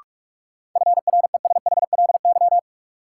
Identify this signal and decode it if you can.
Morse code, heard as a keyed tone, about 46 wpm, 700 Hz. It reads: FRESHLY
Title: Sets of 5 Words Encore - New Word Drill - 45wpm